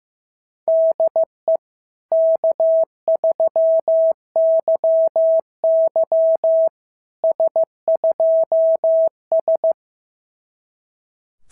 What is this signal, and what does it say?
Morse code, heard as a keyed tone, about 15 wpm, 650 Hz. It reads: DE K3YY S2S
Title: SOTA QSO Practice 15wpm